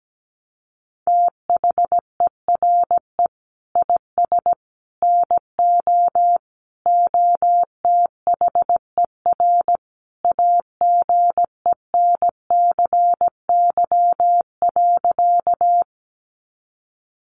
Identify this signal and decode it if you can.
Morse code, heard as a keyed tone, about 17 wpm, 700 Hz. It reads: THERE IS NO OTHER AGENCY.
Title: Sentences from Top 1300 Words 17wpm